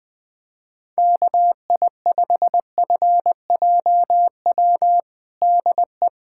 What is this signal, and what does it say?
Morse code, heard as a keyed tone, about 20 wpm, 700 Hz. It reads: KI5FJW DE
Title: USA Callsign DE Callsign - 6 Characters - 20wpm